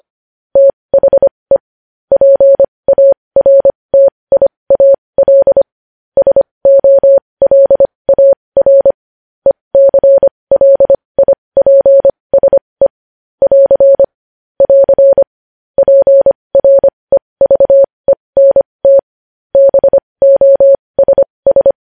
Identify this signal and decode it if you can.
Morse code, heard as a keyed tone, about 25 wpm, spaced as for 20 wpm, 550 Hz. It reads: THE PARTIAL SOLAR ECLIPSE <AR> <AR> PREVENT BOSS
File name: podcast / News headlines-1x-25x20